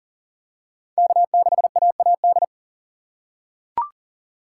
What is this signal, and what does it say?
Morse code, heard as a keyed tone, about 40 wpm, 700 Hz. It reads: K6AAD
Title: USA Call Signs - 5 Characters - 4x Character Spacing - 40wpm